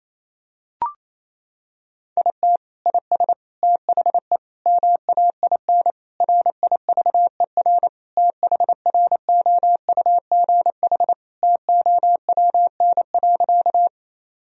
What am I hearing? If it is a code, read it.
Morse code, heard as a keyed tone, about 27 wpm, 700 Hz. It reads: IT IS THE MAIN RIVER THROUGH TOWN.
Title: Sentences from Top 1000 Words - Review Part 2 - 28wpm